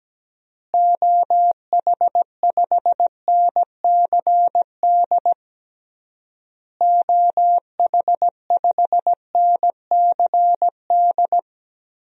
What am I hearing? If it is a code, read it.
Morse code, heard as a keyed tone, about 17 wpm, 700 Hz. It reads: OH5NCD OH5NCD
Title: DXCC-100 Call Signs - 6 Characters - Repeated 3x - 17wpm